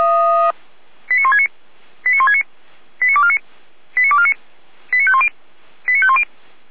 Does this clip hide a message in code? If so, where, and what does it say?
Yes: five-tone selective call (ZVEI) from 1.1 to 6.2 s, heard as quick runs of five tones, tones of about 70 ms. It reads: 89169 89169 89249 89249 89510 89510